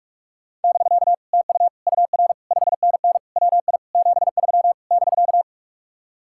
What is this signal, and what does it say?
Morse code, heard as a keyed tone, about 44 wpm, 700 Hz. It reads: <BK> TU UR 5NN WI 73 <BK>